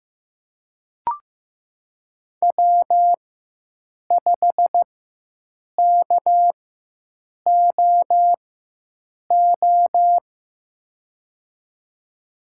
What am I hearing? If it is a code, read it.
Morse code, heard as a keyed tone, about 15 wpm, spaced as for 7 wpm, 700 Hz. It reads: W5KOO
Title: USA Call Signs - 5 Characters - 4x Character Spacing - 15wpm